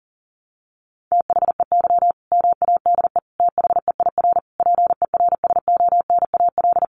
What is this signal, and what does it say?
Morse code, heard as a keyed tone, about 40 wpm, 700 Hz. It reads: THEY MADE THEIR PERSONAL
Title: Sentences from Top 1100 Words 40wpm